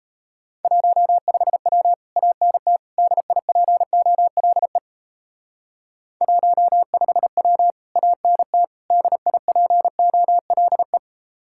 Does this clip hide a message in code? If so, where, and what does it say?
Yes: Morse code from 0.6 to 11.0 s, heard as a keyed tone, about 37 wpm, 700 Hz. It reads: 15W ANT DIPOLE 15W ANT DIPOLE